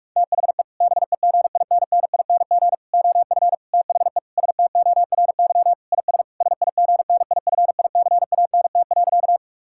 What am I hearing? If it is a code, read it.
Morse code, heard as a keyed tone, about 46 wpm, 700 Hz. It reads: THE BEGINNING OF THE STORY IS SIGNIFICANT.
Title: Sentences from Top 1000 Words - Review Part 6 - 45wpm